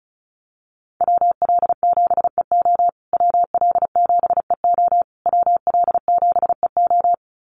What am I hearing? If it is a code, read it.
Morse code, heard as a keyed tone, about 34 wpm, 700 Hz. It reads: WL7EO WL7EO WL7EO
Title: USA Call Signs Repeated 3x 35wpm